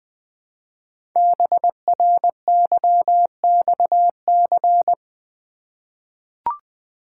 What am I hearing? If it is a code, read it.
Morse code, heard as a keyed tone, about 20 wpm, 700 Hz. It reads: BRYXC
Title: Code Groups - 5 Letters - 20wpm